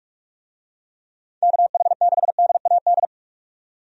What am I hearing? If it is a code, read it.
Morse code, heard as a keyed tone, about 45 wpm, 700 Hz. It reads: KH6DAD
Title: DXCC-100 Call Signs - 6 Characters - 45wpm